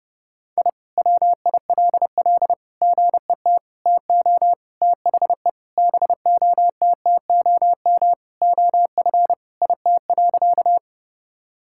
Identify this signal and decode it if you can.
Morse code, heard as a keyed tone, about 30 wpm, 700 Hz. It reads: I WILL GET TO THE BOTTOM OF IT.
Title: Sentences from Top 1000 Words - Review Part 6 - 30wpm